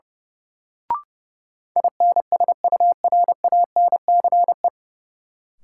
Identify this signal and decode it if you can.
Morse code, heard as a keyed tone, about 31 wpm, 700 Hz. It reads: INSURANCE
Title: Sentences from Top 1100 Words - New Word Drill - 30wpm